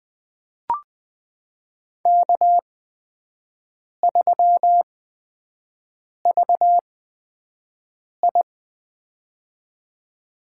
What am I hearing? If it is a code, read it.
Morse code, heard as a keyed tone, about 20 wpm, spaced as for 5 wpm, 700 Hz. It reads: K3VI
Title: USA Call Signs - 8x Character Spacing - 20wpm